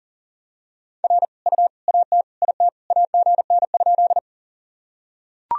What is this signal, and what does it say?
Morse code, heard as a keyed tone, about 40 wpm, 700 Hz. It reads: R U AT IT AGN?
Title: Sentences from Top 200 Words and Abbreviations 40wpm